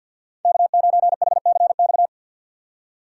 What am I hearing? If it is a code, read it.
Morse code, heard as a keyed tone, about 49 wpm, 700 Hz. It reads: K9HCX
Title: USA Call Signs - 3x Character Spacing - 50wpm